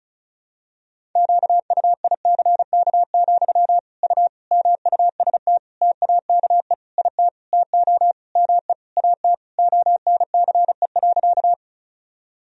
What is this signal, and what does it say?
Morse code, heard as a keyed tone, about 34 wpm, 700 Hz. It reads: QUICK, U MUST TAKE IT TO ME AT ONCE.
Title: Sentences from Top 500 Words and Abbreviations 35wpm